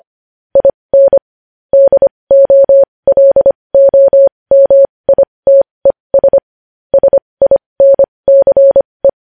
Morse code, heard as a keyed tone, about 25 wpm, spaced as for 20 wpm, 550 Hz. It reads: IN DOLOMITES SINCE
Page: news-headlines-in-morse-code-at-20-wpm